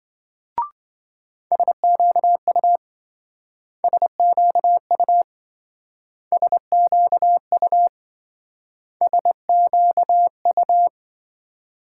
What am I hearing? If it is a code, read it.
Morse code, heard as a keyed tone, about 26 wpm, 700 Hz. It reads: SQU SQU SQU SQU